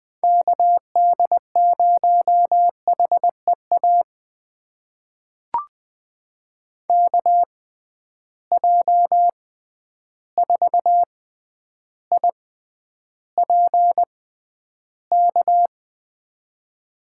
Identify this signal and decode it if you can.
Morse code, heard as a keyed tone, about 20 wpm, spaced as for 10 wpm, 700 Hz. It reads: KD0HEA KJ4IPK
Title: USA Call Signs - 6 Characters - 6x Character Spacing - 20wpm